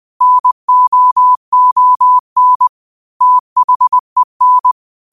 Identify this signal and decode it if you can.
Morse code, heard as a keyed tone, about 20 wpm, 1000 Hz. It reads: NOON THEN
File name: qotdfiles / 20wpm